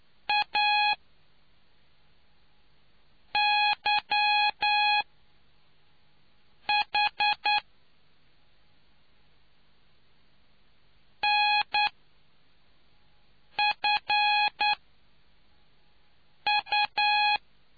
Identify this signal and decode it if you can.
Morse code, heard as a keyed tone, about 9 wpm, 830 Hz. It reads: A Y H N F U